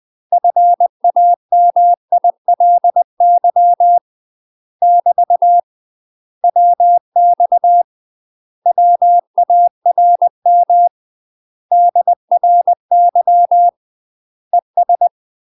Morse code, heard as a keyed tone, about 20 wpm, 700 Hz. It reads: FAMILY <BT> WX WARM DRY ES
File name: UK MASTER QSOs TRAINING/20WPM